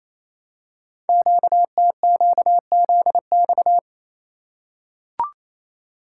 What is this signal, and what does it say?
Morse code, heard as a keyed tone, about 28 wpm, 700 Hz. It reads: QTQZX
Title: Code Groups - 5 Letters - 28wpm